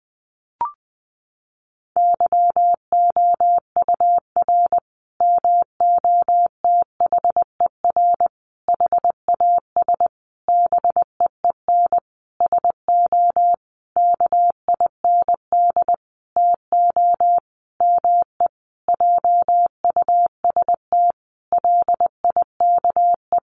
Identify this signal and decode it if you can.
Morse code, heard as a keyed tone, about 20 wpm, 700 Hz. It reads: YOUR MOTHER HAS BEEN SO KIND TO ME JUST LIKE
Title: Sentences from Top 1200 Words 20wpm